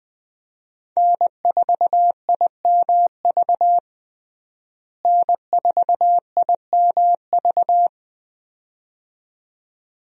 Morse code, heard as a keyed tone, about 20 wpm, 700 Hz. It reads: N4IMV N4IMV
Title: USA Call Signs - 5 Characters - Repeated 2x - 20wpm